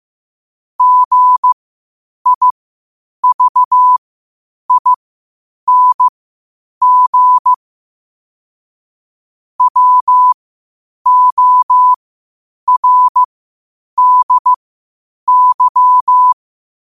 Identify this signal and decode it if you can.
Morse code, heard as a keyed tone, about 15 wpm, spaced as for 8 wpm, 1000 Hz. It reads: GIVING WORDY
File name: qotdfiles / 8wpm